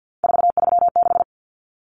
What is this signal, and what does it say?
Morse code, heard as a keyed tone, about 50 wpm, 700 Hz. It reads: 436
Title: Numbers - 3-Digit 50wpm